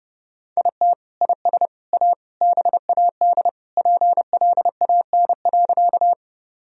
Morse code, heard as a keyed tone, about 30 wpm, 700 Hz. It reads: IT IS A BAD PLAN.
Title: Sentences from Top 1000 Words - Review Part 2 - 30wpm